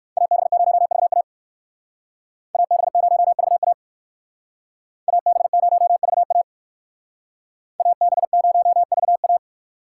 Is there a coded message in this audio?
Yes: Morse code from 0.2 to 9.4 s, heard as a keyed tone, about 62 wpm, 700 Hz. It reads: AB0VA AB0VA AB0VA AB0VA